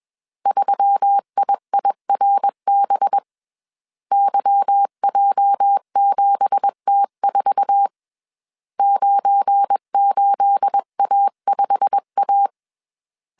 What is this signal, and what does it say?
Morse code, heard as a keyed tone, about 21 wpm, 800 Hz. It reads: 3IIRB YJ7T4 98A5A
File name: CW / DET / 1 9 1500C S B 20wpm 2313